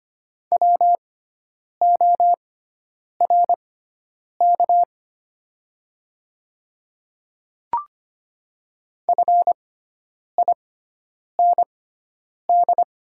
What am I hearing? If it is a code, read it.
Morse code, heard as a keyed tone, about 25 wpm, spaced as for 9 wpm, 700 Hz. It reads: WORK FIND
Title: 6x Character Spacing - Top 200 Words - 25wpm